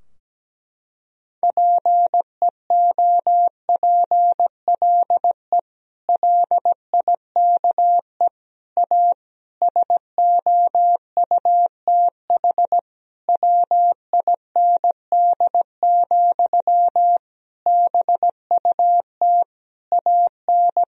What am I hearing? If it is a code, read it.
Morse code, heard as a keyed tone, about 17 wpm, 700 Hz. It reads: PEOPLE LIKE A SOUTH WIND, BUT AN
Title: Sentences from Top 1000 Words - Review Part 3 - 17wpm